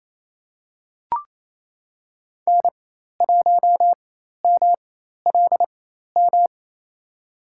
Morse code, heard as a keyed tone, about 28 wpm, spaced as for 13 wpm, 700 Hz. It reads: N1MLM
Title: USA Call Signs - 4x Character Spacing - 28wpm